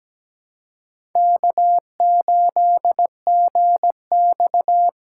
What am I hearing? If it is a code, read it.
Morse code, heard as a keyed tone, about 17 wpm, 700 Hz. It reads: K8GX
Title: USA Call Signs - 4 Characters - 2x Character Spacing - 17wpm